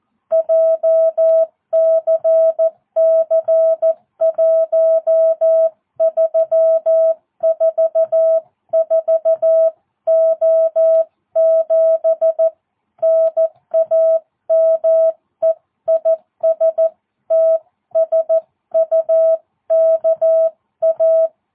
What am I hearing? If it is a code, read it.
Morse code, heard as a keyed tone, about 14 wpm, 650 Hz. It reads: JCC1344O7 NAME IS TSUKA
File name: jh8cbh / 2022cw